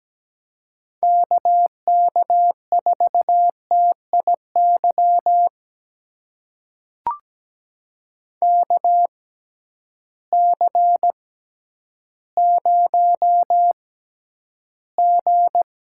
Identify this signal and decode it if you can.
Morse code, heard as a keyed tone, about 17 wpm, spaced as for 10 wpm, 700 Hz. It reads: KK4TIY KC0G
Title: USA Call Signs - 6 Characters - 6x Character Spacing - 17wpm